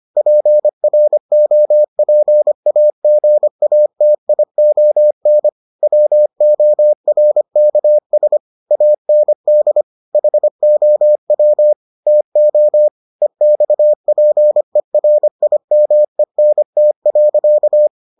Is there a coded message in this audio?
Yes: Morse code from 0.2 to 17.9 s, heard as a keyed tone, about 25 wpm, 600 Hz. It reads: PROPAGATION WORKS AND HOW TO EXPERIMENT.